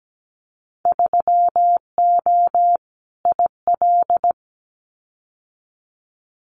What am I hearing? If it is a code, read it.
Morse code, heard as a keyed tone, about 17 wpm, 690 Hz. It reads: 3O IL
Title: Winter Field Day Exchanges 17wpm